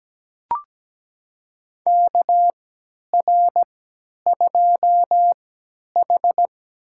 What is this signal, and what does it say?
Morse code, heard as a keyed tone, about 17 wpm, spaced as for 10 wpm, 700 Hz. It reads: KR2H
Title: USA Call Signs - 4 Characters - 3x Character Spacing - 17wpm